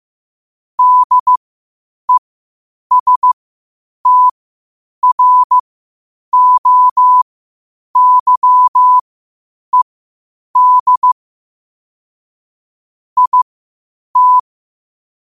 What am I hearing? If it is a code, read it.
Morse code, heard as a keyed tone, about 15 wpm, spaced as for 8 wpm, 1000 Hz. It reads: DESTROYED IT